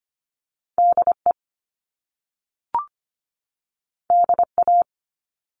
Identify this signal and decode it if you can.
Morse code, heard as a keyed tone, about 25 wpm, 700 Hz. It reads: DE DA
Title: Top Two Letter Combinations in Words 25wpm